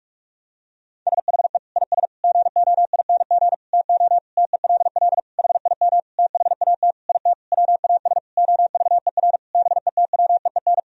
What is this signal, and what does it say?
Morse code, heard as a keyed tone, about 47 wpm, 700 Hz. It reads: SHE IS GOING TO TELL HIM THAT IT WAS OVER BETWEEN
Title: Sentences from Top 1000 Words - Review Part 1 - 45wpm